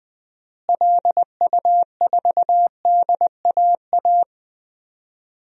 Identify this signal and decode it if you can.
Morse code, heard as a keyed tone, about 20 wpm, 700 Hz. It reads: LU4DAA
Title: DXCC-100 Call Signs - 6 Characters - Repeated 3x - 20wpm